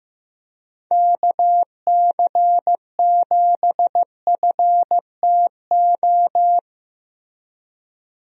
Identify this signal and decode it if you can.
Morse code, heard as a keyed tone, about 15 wpm, 700 Hz. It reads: KC7FTO
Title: USA Call Signs - 6 Characters - Repeated 2x - 15wpm